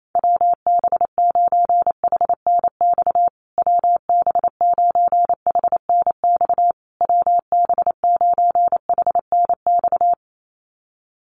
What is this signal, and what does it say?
Morse code, heard as a keyed tone, about 28 wpm, 700 Hz. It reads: WB9HNX WB9HNX WB9HNX
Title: USA Call Signs Repeated 3x 28wpm